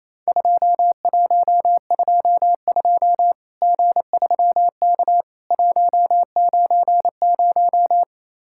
Morse code, heard as a keyed tone, about 28 wpm, 700 Hz. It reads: 2122 G3K 190
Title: Worked All Europe DX Contest - QTC Exchange - 28wpm